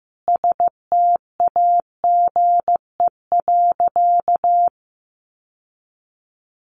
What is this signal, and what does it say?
Morse code, heard as a keyed tone, about 15 wpm, 700 Hz. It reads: STAGE.
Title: Sentences from Top 1200 Words 15wpm